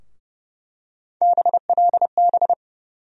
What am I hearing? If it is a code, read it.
Morse code, heard as a keyed tone, about 31 wpm, 700 Hz. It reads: BLB
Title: USA Call Sign Postfixes - 3 Letters 30wpm